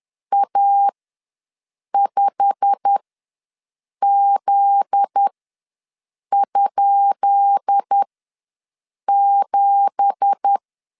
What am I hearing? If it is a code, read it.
Morse code, heard as a keyed tone, about 10 wpm, spaced as for 6 wpm, 800 Hz. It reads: A5Z?7
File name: LEZIONI / 10 SIGNS / 10B INT ws 10wpm 4134